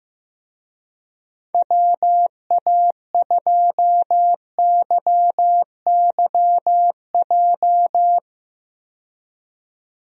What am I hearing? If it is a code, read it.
Morse code, heard as a keyed tone, about 15 wpm, 700 Hz. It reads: WA2YYJ
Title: USA Call Signs - 6 Characters - 15wpm